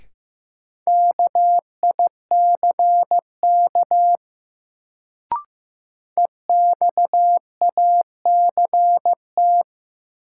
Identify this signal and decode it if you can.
Morse code, heard as a keyed tone, about 15 wpm, 700 Hz. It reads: KICK EXACT
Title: Sentences from Top 1300 Words - New Word Drill - 15wpm